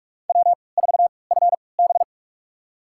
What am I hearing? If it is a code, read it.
Morse code, heard as a keyed tone, about 44 wpm, spaced as for 25 wpm, 700 Hz. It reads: W4FB
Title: USA Call Signs - 3x Character Spacing - 45wpm